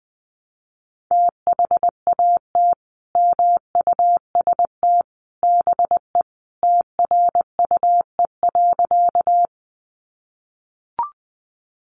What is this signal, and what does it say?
Morse code, heard as a keyed tone, about 20 wpm, 700 Hz. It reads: THAT MUST BE TRUE.
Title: Sentences from Top 400 Words 20wpm